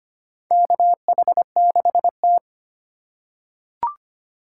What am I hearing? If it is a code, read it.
Morse code, heard as a keyed tone, about 25 wpm, 700 Hz. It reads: KH6T
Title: USA Call Signs - 4 Characters - 4x Character Spacing - 25wpm